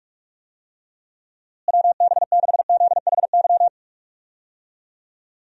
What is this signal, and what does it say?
Morse code, heard as a keyed tone, about 43 wpm, 700 Hz. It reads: WB6ZHY
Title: USA Call Signs - 6 Characters - Repeated 3x - 45wpm